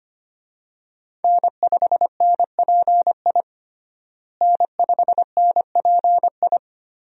Morse code, heard as a keyed tone, about 25 wpm, 700 Hz. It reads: N5NPI N5NPI
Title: USA Call Signs - 5 Characters - Repeated 2x - 25wpm